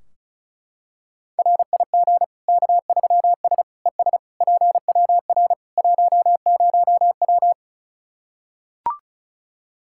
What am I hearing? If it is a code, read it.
Morse code, heard as a keyed tone, about 34 wpm, 700 Hz. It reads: RIG K3S ES PWR 10W
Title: Practice QSO - Exchange 2 - Info - 35wpm